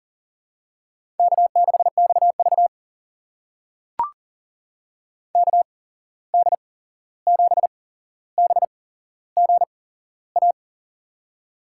Morse code, heard as a keyed tone, about 40 wpm, spaced as for 15 wpm, 700 Hz. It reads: K6XV KD7BGA
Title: USA Call Signs - 8x Character Spacing - 40wpm